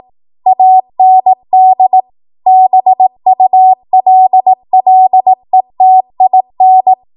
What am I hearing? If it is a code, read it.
Morse code, heard as a keyed tone, about 18 wpm, 750 Hz. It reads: AND BULLETIN
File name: Archive / 18 WPM / 18WPM